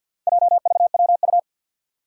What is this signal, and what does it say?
Morse code, heard as a keyed tone, about 53 wpm, 700 Hz. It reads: JVWU
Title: Four Challenging Letter-Number Combinations 50wpm